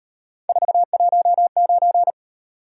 Morse code, heard as a keyed tone, about 35 wpm, 700 Hz. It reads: 419